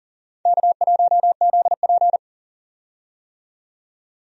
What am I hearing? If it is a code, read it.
Morse code, heard as a keyed tone, about 40 wpm, 700 Hz. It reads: K1ZP
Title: USA Call Signs - 4 Characters - 40wpm